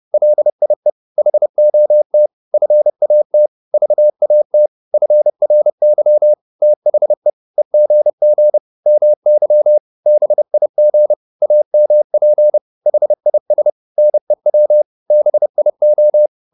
Morse code, heard as a keyed tone, about 30 wpm, 600 Hz. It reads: LIE HOT FAT VAT FRY THE EGG MY BIG AMP HIS NEW BIO